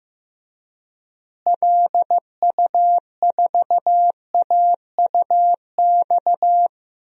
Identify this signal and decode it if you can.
Morse code, heard as a keyed tone, about 15 wpm, 700 Hz. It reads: LU4AUX
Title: DXCC-100 Call Signs - 6 Characters - Repeated 2x - 15wpm